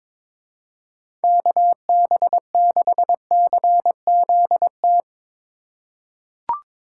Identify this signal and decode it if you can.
Morse code, heard as a keyed tone, about 22 wpm, 700 Hz. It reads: KB6CZT
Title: USA Call Signs - 6 Characters - 2x Character Spacing - 22wpm